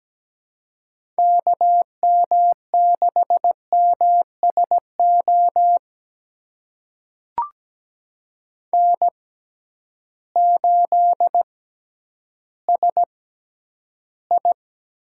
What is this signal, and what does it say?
Morse code, heard as a keyed tone, about 17 wpm, spaced as for 10 wpm, 700 Hz. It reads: KM6MSO N8SI
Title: USA Call Signs - 6x Character Spacing - 17wpm